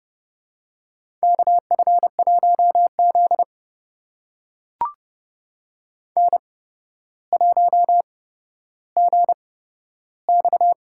Morse code, heard as a keyed tone, about 30 wpm, spaced as for 13 wpm, 700 Hz. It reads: KF1Z N1GX
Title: USA Call Signs - 4 Characters - 8x Character Spacing - 30wpm